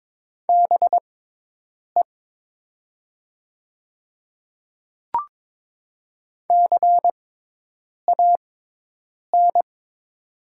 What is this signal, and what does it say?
Morse code, heard as a keyed tone, about 22 wpm, spaced as for 8 wpm, 700 Hz. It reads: BE CAN
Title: 6x Character Spacing - Top 100 Words - 22wpm